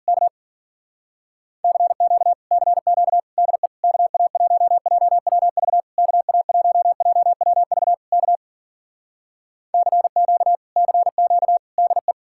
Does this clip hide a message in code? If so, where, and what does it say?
Yes: Morse code from 0.1 to 12.1 s, heard as a keyed tone, about 55 wpm, 700 Hz. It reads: K CQ CQ DE KA1JWV KA1JWV K CQ CQ DE